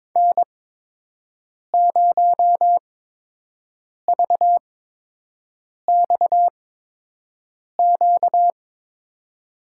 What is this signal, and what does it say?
Morse code, heard as a keyed tone, about 22 wpm, spaced as for 6 wpm, 700 Hz. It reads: N0VXQ
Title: USA Call Signs - 5 Characters - 8x Character Spacing - 22wpm